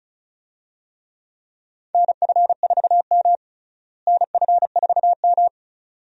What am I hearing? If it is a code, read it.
Morse code, heard as a keyed tone, about 36 wpm, 700 Hz. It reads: NF4M NF4M
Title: USA Call Signs - 4 Characters - Repeated 2x - 35wpm